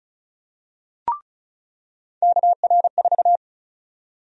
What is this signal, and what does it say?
Morse code, heard as a keyed tone, about 34 wpm, 700 Hz. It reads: KR4
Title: USA Call Sign Prefixes - 35wpm